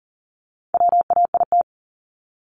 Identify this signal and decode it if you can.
Morse code, heard as a keyed tone, about 40 wpm, 700 Hz. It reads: WAIT
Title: ING Suffix - Top 100 Words with Base Words 40wpm